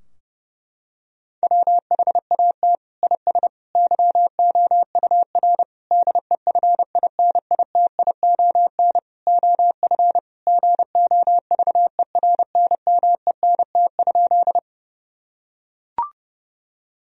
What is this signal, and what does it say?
Morse code, heard as a keyed tone, about 32 wpm, 700 Hz. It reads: WHAT IS YOUR DEFINITION OF GOVERNMENT?
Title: Sentences from Top 1000 Words - Review Part 4 - 30wpm